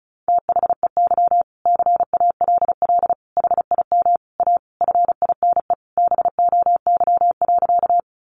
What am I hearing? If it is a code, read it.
Morse code, heard as a keyed tone, about 34 wpm, 700 Hz. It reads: THEY CALL HIM A FINE BOY.
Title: Sentences from Top 500 Words 35wpm